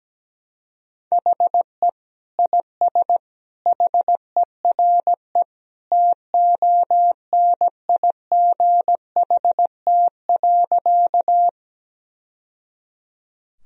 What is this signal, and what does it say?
Morse code, heard as a keyed tone, about 17 wpm, 700 Hz. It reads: HE IS HERE TONIGHT.
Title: Sentences from Top 1000 Words - Review Part 3 - 17wpm